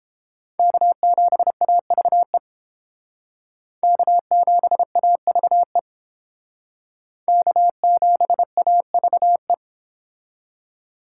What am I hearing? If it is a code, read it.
Morse code, heard as a keyed tone, about 30 wpm, 700 Hz. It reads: K7AVE K7AVE K7AVE